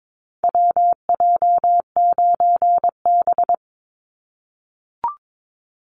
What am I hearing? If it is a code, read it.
Morse code, heard as a keyed tone, about 22 wpm, 700 Hz. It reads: WJ9B
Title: Call Signs - 22wpm